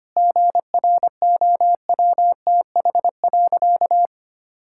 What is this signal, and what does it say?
Morse code, heard as a keyed tone, about 25 wpm, 700 Hz. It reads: GROWTH.
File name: practice / 25wpm / Difficult Sentences with Hints - 25wpm